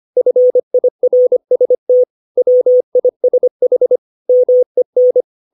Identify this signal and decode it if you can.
Morse code, heard as a keyed tone, about 25 wpm, 500 Hz. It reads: FIRST WISH MEN
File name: 632 words / 632 Words 25WPM